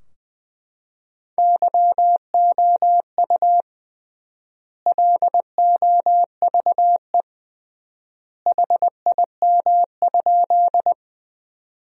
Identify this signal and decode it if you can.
Morse code, heard as a keyed tone, about 20 wpm, 700 Hz. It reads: YOU LOVE HIM?
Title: 3x Word-Spacing - Sentences from Top 400 Words - 20wpm